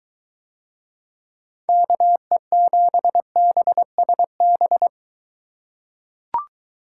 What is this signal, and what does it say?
Morse code, heard as a keyed tone, about 23 wpm, 700 Hz. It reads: KE7BSB